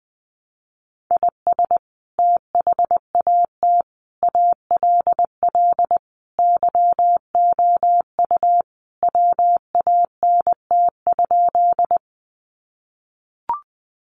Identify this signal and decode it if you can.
Morse code, heard as a keyed tone, about 20 wpm, 700 Hz. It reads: IS THAT ALL YOU WANT?
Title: Sentences from Top 200 Words 20wpm